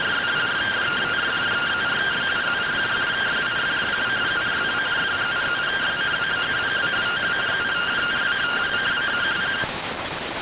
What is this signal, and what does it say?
Two-tone teleprinter signal, about 170 Hz shift.